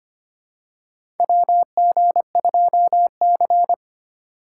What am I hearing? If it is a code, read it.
Morse code, heard as a keyed tone, about 25 wpm, 700 Hz. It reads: WG2C
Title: USA Call Signs - 4 Characters - 2x Character Spacing - 25wpm